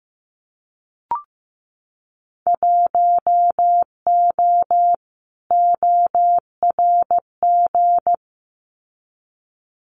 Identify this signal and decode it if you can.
Morse code, heard as a keyed tone, about 15 wpm, 700 Hz. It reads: 1O ORG
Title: Winter Field Day Exchanges 15wpm